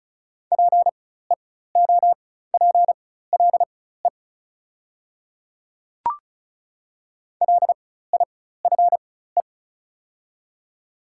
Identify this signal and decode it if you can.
Morse code, heard as a keyed tone, about 36 wpm, spaced as for 16 wpm, 700 Hz. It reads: PEOPLE LIFE
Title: 4x Character Spacing - Top 300 Words - 35wpm